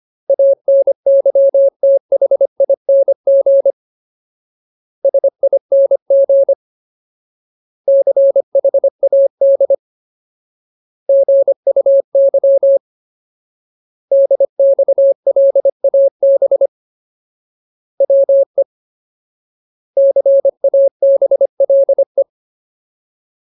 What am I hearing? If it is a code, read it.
Morse code, heard as a keyed tone, about 25 wpm, 550 Hz. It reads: ANYTHING SING CHAD GUY DXLAB WE CABLE
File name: Ultimate Word List / 25WPM Ultimate